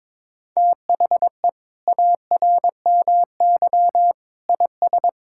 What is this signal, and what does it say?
Morse code, heard as a keyed tone, about 22 wpm, 700 Hz. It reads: THE ARMY IS
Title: Sentences from Top 1000 Words - Review Part 6 - 22wpm